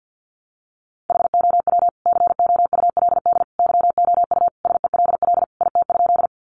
Morse code, heard as a keyed tone, about 50 wpm, 700 Hz. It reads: HOW COULD YOU SELL IT?
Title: Sentences from Top 1100 Words 50wpm